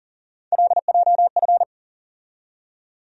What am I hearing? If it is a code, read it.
Morse code, heard as a keyed tone, about 40 wpm, 700 Hz. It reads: LJF
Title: USA Call Sign Postfixes - 3 Letters 40wpm